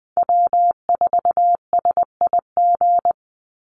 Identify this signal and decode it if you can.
Morse code, heard as a keyed tone, about 20 wpm, 700 Hz. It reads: W4SIG
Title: Call Signs - 20wpm